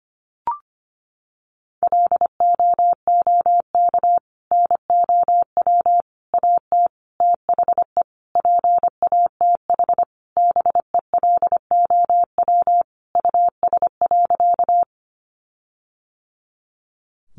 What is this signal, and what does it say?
Morse code, heard as a keyed tone, about 25 wpm, 700 Hz. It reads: LOOK NOW AT THE PATH BELOW US.
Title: Sentences from Top 1400 Words 25wpm